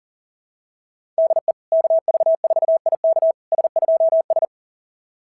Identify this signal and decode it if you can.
Morse code, heard as a keyed tone, about 44 wpm, 660 Hz. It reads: DE KV4IK S2S
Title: SOTA QSO Practice 40wpm